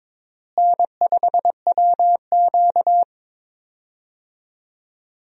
Morse code, heard as a keyed tone, about 22 wpm, 700 Hz. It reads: N5WQ
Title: USA Call Signs - 4 Characters - Repeated 2x - 22wpm